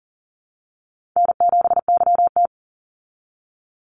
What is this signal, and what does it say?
Morse code, heard as a keyed tone, about 39 wpm, 700 Hz. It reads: N7YT
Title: Easy Call Signs Repeated 3x - 40wpm